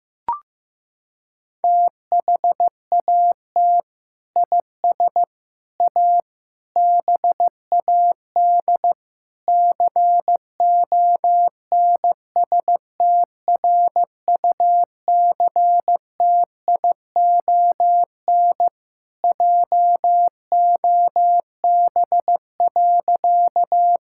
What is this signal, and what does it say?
Morse code, heard as a keyed tone, about 15 wpm, 700 Hz. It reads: THAT IS A BAD CONSTRUCTION JOB.
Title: Sentences from Top 1600 Words 15wpm